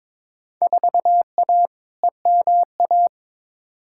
Morse code, heard as a keyed tone, about 22 wpm, 700 Hz. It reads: 4A EMA
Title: Field Day Exchanges - 22wpm